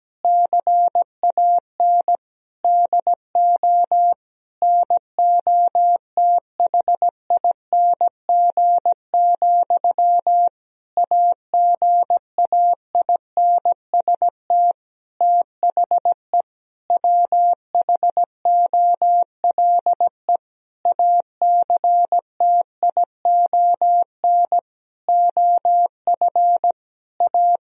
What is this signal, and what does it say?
Morse code, heard as a keyed tone, about 17 wpm, 700 Hz. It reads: CAN DO NOTHING, AGAINST THE WHOLE ACTION OF A
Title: Sentences from Top 1000 Words - Review Part 4 - 17wpm